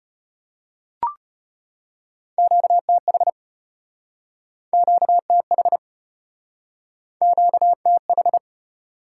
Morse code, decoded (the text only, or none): QTH QTH QTH